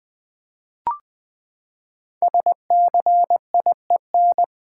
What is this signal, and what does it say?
Morse code, heard as a keyed tone, about 20 wpm, 700 Hz. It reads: SCIEN
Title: Build A Word - 5 Letters At A Time - 10-Letter Words - 20wpm